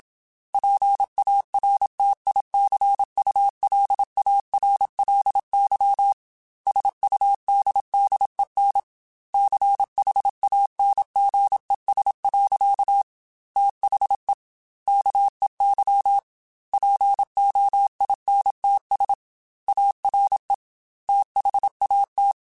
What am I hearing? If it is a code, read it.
Morse code, heard as a keyed tone, about 27 wpm, 800 Hz. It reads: PARTICULARLY SUDDEN CHANGES. THE KEY POINTS ARE THAT
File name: Technical Subjects / Technical Subjects-mp3s / Reactance @25